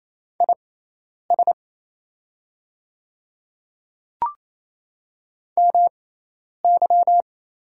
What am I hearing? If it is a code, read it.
Morse code, heard as a keyed tone, about 28 wpm, spaced as for 10 wpm, 700 Hz. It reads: IS MY
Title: 6x Character Spacing - Top 200 Words - 28wpm